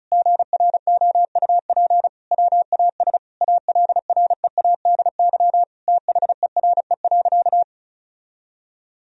Morse code, heard as a keyed tone, about 36 wpm, 700 Hz. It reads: GROUP WAS ALREADY THERE.
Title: Sentences from Top 1000 Words - Review Part 1 - 35wpm